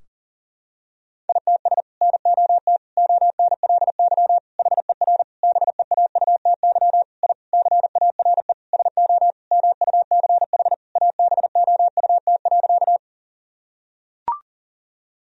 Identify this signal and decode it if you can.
Morse code, heard as a keyed tone, about 40 wpm, 700 Hz. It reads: ITS NOT ONLY HER BEAUTY I CARE SO MUCH ABOUT.
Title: Sentences from Top 1000 Words - Review Part 2 - 40wpm